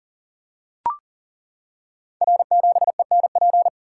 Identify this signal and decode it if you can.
Morse code, heard as a keyed tone, about 39 wpm, 700 Hz. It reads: R7ENP
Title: Code Groups - 5 Letter:Numbers - 40wpm